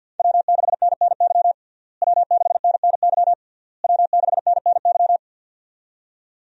Morse code, heard as a keyed tone, about 51 wpm, 700 Hz. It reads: W6NNY W6NNY W6NNY
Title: USA Call Signs - 5 Characters - Repeated 3x - 50wpm